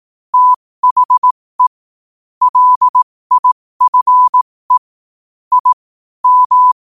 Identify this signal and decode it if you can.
Morse code, heard as a keyed tone, about 18 wpm, spaced as for 15 wpm, 1000 Hz. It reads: THE LIFE I M
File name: qotdfiles / 15wpm